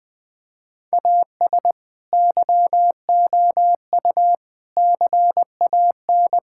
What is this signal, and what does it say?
Morse code, heard as a keyed tone, about 20 wpm, 700 Hz. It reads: AS YOU CAN
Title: Sentences with 7-Word Phrases - Spoken Beginning - 20wpm